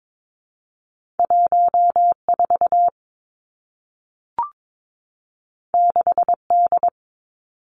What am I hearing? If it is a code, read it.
Morse code, heard as a keyed tone, about 22 wpm, 700 Hz. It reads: 1 4 6 D
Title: Two Challenging Letter-Number Combinations 22wpm